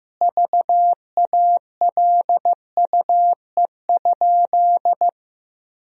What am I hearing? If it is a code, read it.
Morse code, heard as a keyed tone, about 15 wpm, 700 Hz. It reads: VALUE?
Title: Sentences from Top 1000 Words - Review Part 3 - 15wpm